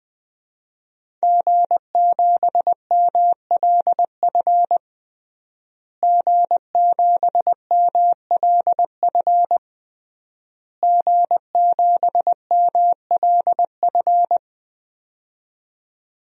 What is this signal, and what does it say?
Morse code, heard as a keyed tone, about 20 wpm, 700 Hz. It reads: G7MLF G7MLF G7MLF
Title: DXCC-100 Call Signs - 5 Characters - Repeated 3x - 20wpm